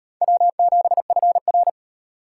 Morse code, heard as a keyed tone, about 38 wpm, 700 Hz. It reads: W7FR